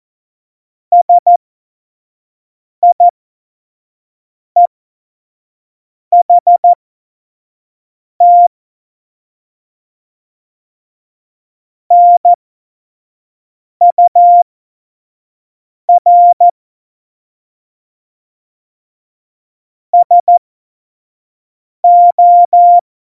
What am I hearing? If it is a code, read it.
Morse code, heard as a keyed tone, about 14 wpm, spaced as for 5 wpm, 700 Hz. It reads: SIEHT NUR SO
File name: Morsekurs / Witze / Witz 24.Dezember 2019, 25 F70